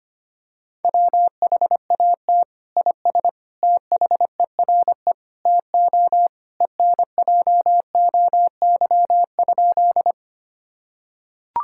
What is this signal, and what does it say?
Morse code, heard as a keyed tone, about 25 wpm, 700 Hz. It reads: WHAT IS THERE TO ENJOY?
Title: Sentences from Top 1000 Words - Review Part 5 - 25wpm